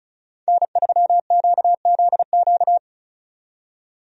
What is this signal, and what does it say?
Morse code, heard as a keyed tone, about 34 wpm, 700 Hz. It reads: N3QZQ
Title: USA Call Signs - 5 Characters - 35wpm